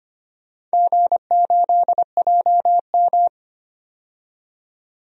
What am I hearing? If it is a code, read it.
Morse code, heard as a keyed tone, about 25 wpm, 690 Hz. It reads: G8JM
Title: DXCC-100 Call Signs - 4 Characters - 25wpm